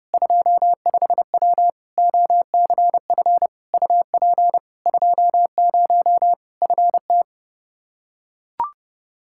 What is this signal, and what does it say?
Morse code, heard as a keyed tone, about 29 wpm, 700 Hz. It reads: 25W OCF UP 20 FT
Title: K1USN SST QSO Practice 30wpm